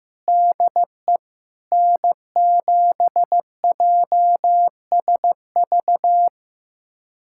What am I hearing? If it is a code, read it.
Morse code, heard as a keyed tone, about 15 wpm, 700 Hz. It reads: DE N7JSV
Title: USA Callsign DE Callsign - 5 Characters - 15wpm